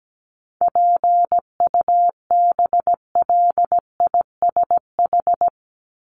Morse code, heard as a keyed tone, about 17 wpm, 700 Hz. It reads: PUBLISH
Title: ED Suffix - Top 100 Words with Base Words 17wpm